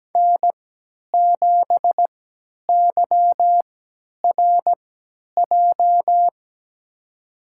Morse code, heard as a keyed tone, about 17 wpm, spaced as for 10 wpm, 700 Hz. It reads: N7YRJ
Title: USA Call Signs - 3x Character Spacing - 17wpm